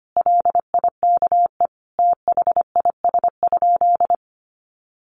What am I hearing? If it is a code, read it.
Morse code, heard as a keyed tone, about 25 wpm, 700 Hz. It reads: LIKE THIS?
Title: Sentences from Top 600 Words 25wpm